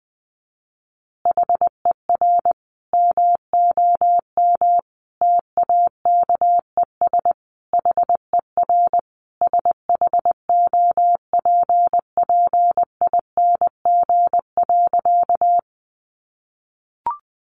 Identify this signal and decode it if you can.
Morse code, heard as a keyed tone, about 20 wpm, 700 Hz. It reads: HER MOM TAKES HER SHOPPING.
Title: Additional Easy Phrases 20wpm